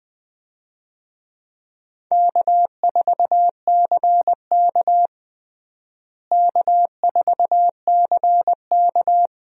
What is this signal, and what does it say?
Morse code, heard as a keyed tone, about 20 wpm, 700 Hz. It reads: K4CK K4CK
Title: USA Call Signs - 4 Characters - Repeated 2x - 20wpm